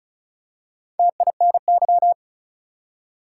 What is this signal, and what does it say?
Morse code, heard as a keyed tone, about 36 wpm, 700 Hz. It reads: TINY
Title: Four Letter Words - 35wpm